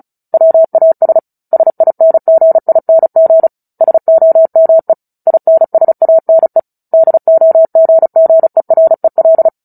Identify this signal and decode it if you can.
Morse code, heard as a keyed tone, about 34 wpm, 650 Hz. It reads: WAS SINGING SOME INSANE DOGGEREL
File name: Book Two / 09-Chapter 9 - 35 WPM